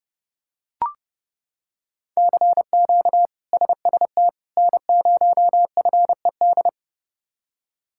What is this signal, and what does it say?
Morse code, heard as a keyed tone, about 28 wpm, 700 Hz. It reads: CQ SST N0FED
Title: K1USN SST QSO Practice - 30wpm